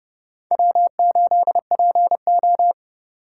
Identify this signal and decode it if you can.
Morse code, heard as a keyed tone, about 30 wpm, 700 Hz. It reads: W8PO